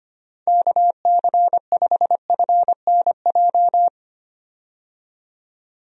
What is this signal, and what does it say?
Morse code, heard as a keyed tone, about 25 wpm, 700 Hz. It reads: KC5FNJ
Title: USA Call Signs - 6 Characters - 25wpm